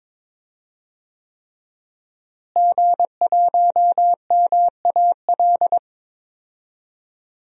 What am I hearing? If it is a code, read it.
Morse code, heard as a keyed tone, about 23 wpm, 700 Hz. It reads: G1MAL
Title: DXCC-100 Call Signs - 5 Characters - Repeated 3x - 22wpm